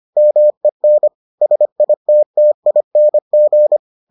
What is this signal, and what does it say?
Morse code, heard as a keyed tone, about 25 wpm, 600 Hz. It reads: MEN SITTING